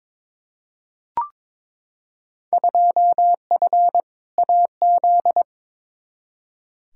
Morse code, heard as a keyed tone, about 22 wpm, 700 Hz. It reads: 2F AZ
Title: Field Day Exchanges - 22wpm